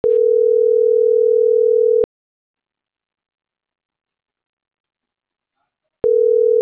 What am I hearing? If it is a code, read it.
Ringback tone (North America), 440 + 480 Hz.